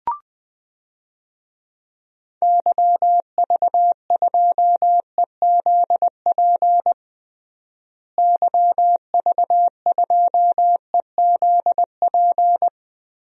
Morse code, heard as a keyed tone, about 20 wpm, 700 Hz. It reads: YV2EZP YV2EZP
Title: DXCC-100 Call Signs - 6 Characters - Repeated 3x - 20wpm